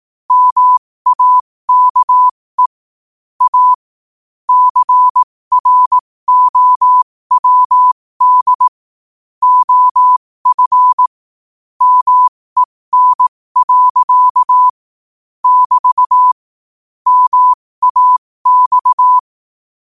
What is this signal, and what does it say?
Morse code, heard as a keyed tone, about 18 wpm, spaced as for 15 wpm, 1000 Hz. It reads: MAKE A CROWD OF MEN. <BT> MAX
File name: qotdfiles / 15wpm